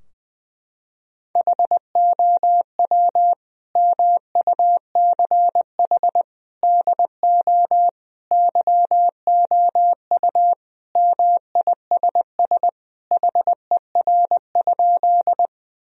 Morse code, heard as a keyed tone, about 20 wpm, 700 Hz. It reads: HOW MUCH DO YOU MISS HER?
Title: Sentences from Top 1000 Words - Review Part 3 - 20wpm